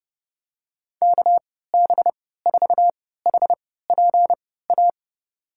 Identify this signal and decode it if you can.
Morse code, heard as a keyed tone, about 30 wpm, spaced as for 17 wpm, 700 Hz. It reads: KB4HPA
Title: USA Call Signs - 6 Characters - 3x Character Spacing - 30wpm